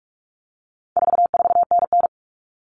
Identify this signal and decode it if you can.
Morse code, heard as a keyed tone, about 44 wpm, 700 Hz. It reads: 44NN
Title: Common Cut Numbers - 4-Digit 45wpm